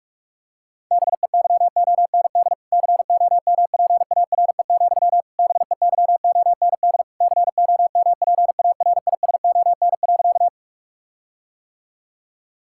Morse code, heard as a keyed tone, about 46 wpm, 700 Hz. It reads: BEYOND COMPARE, BEYOND COMPARISON.
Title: Sentences from Top 1600 Words 45wpm